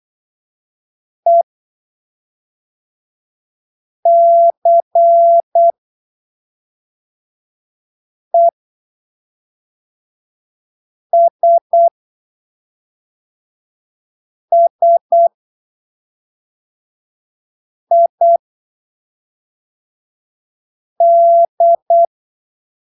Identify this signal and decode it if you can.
Morse code, heard as a keyed tone, about 8 wpm, spaced as for 3 wpm, 680 Hz. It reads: ECESSID
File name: arquivos / texto / texto 23 08ppm